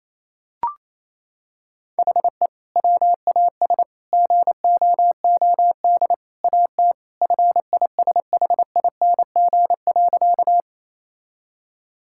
Morse code, heard as a keyed tone, about 28 wpm, 700 Hz. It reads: HE WAS GOOD AT FISHING.
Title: Sentences from Top 1000 Words - Review Part 5 - 28wpm